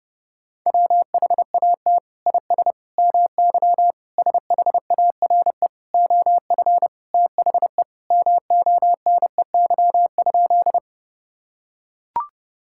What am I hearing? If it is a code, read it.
Morse code, heard as a keyed tone, about 28 wpm, 700 Hz. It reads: WHAT IS MY SHARE OF THE MONEY?
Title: Sentences from Top 1000 Words - Review Part 6 - 30wpm